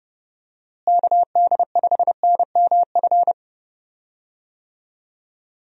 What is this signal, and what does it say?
Morse code, heard as a keyed tone, about 32 wpm, 700 Hz. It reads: KD5NMF
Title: USA Call Signs - 6 Characters - Repeated 3x - 30wpm